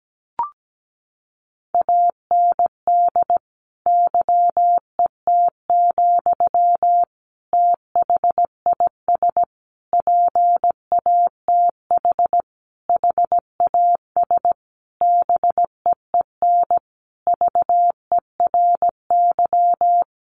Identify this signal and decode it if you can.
Morse code, heard as a keyed tone, about 17 wpm, 700 Hz. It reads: AND YET, THIS PATH HAS BEEN VERY
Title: Sentences from Top 1400 Words 17wpm